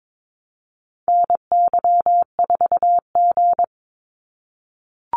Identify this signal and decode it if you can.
Morse code, heard as a keyed tone, about 22 wpm, 700 Hz. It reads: NY4G
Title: NA SOTA Activator Callsigns 22wpm